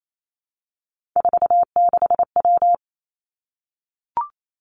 Morse code, heard as a keyed tone, about 28 wpm, 700 Hz. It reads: 46W
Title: Three Challenging Letter-Number Combinations 28wpm